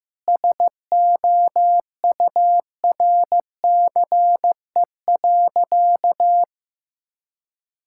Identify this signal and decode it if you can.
Morse code, heard as a keyed tone, about 15 wpm, 700 Hz. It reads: SOURCE.
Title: Sentences from Top 1000 Words - Review Part 5 - 15wpm